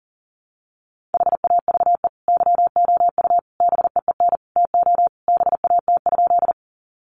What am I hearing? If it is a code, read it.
Morse code, heard as a keyed tone, about 40 wpm, 700 Hz. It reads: HAVE YOU BEEN TO BAT?
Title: Sentences from Top 1500 Words 40wpm